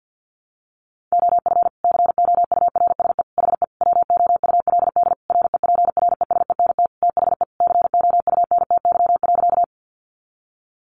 Morse code, heard as a keyed tone, about 48 wpm, 700 Hz. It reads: OF COURSE HE WOULD REPRESENT THE COUNTY.
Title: Sentences from Top 1200 Words 50wpm